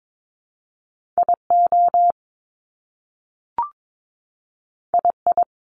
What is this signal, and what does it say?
Morse code, heard as a keyed tone, about 22 wpm, 700 Hz. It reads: IO II
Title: Top Two Letter Combinations in Words 22wpm